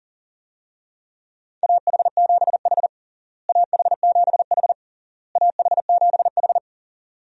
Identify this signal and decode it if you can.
Morse code, heard as a keyed tone, about 40 wpm, 700 Hz. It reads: AH7H AH7H AH7H
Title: USA Call Signs - 4 Characters - Repeated 3x - 40wpm